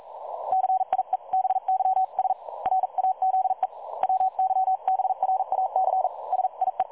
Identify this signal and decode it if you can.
Morse code, heard as a keyed tone, about 42 wpm, 760 Hz. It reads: CIEBYS RAZE WYBRAC SIE
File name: dlazaawansowanych / nagraniazpasma / websdr recording 3565.0kHz